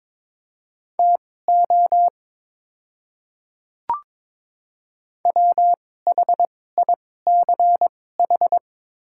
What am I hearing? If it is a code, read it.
Morse code, heard as a keyed tone, about 23 wpm, spaced as for 16 wpm, 700 Hz. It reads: TO WHICH
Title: 2x Character Spacing - Top 100 Words - 22wpm